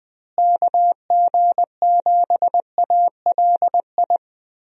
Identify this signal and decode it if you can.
Morse code, heard as a keyed tone, about 20 wpm, 700 Hz. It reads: KG7ALI
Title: USA Call Signs - 6 Characters - Repeated 2x - 20wpm